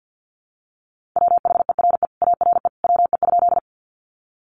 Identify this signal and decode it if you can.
Morse code, heard as a keyed tone, about 48 wpm, 700 Hz. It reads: WHERE ARE WE?
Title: Sentences from Top 200 Words 50wpm